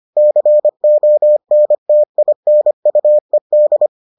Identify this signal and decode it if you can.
Morse code, heard as a keyed tone, about 25 wpm, 600 Hz. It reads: CONTINUED